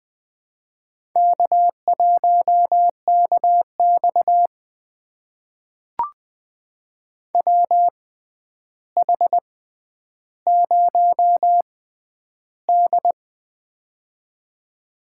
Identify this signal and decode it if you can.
Morse code, heard as a keyed tone, about 20 wpm, spaced as for 10 wpm, 700 Hz. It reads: K1KX WH0D
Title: USA Call Signs - 4 Characters - 6x Character Spacing - 20wpm